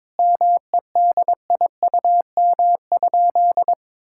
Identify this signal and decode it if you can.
Morse code, heard as a keyed tone, about 22 wpm, 700 Hz. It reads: MEDIUM?
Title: Sentences from Top 1000 Words - Review Part 6 - 22wpm